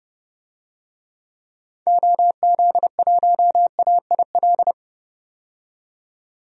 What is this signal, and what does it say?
Morse code, heard as a keyed tone, about 31 wpm, 700 Hz. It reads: OZ1AIL
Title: DXCC-100 Call Signs - 6 Characters - 30wpm